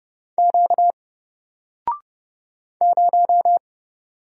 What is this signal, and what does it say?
Morse code, heard as a keyed tone, about 29 wpm, 700 Hz. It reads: Q 0
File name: Single-Letter-Number-30wpm